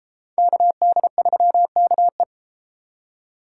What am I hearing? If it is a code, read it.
Morse code, heard as a keyed tone, about 32 wpm, 700 Hz. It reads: KD3KE